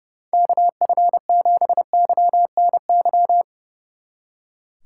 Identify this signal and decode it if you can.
Morse code, heard as a keyed tone, about 39 wpm, 700 Hz. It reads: KF7YNY